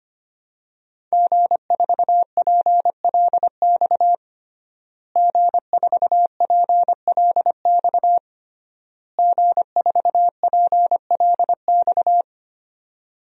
Morse code, heard as a keyed tone, about 25 wpm, 700 Hz. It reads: G4PLX G4PLX G4PLX
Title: DXCC-100 Call Signs - 5 Characters - Repeated 3x - 25wpm